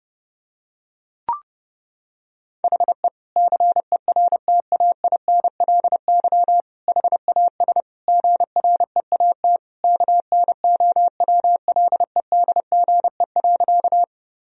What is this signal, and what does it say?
Morse code, heard as a keyed tone, about 28 wpm, 700 Hz. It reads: HE CERTAINLY HAS GREAT KNOWLEDGE.
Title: Sentences from Top 1000 Words - Review Part 3 - 30wpm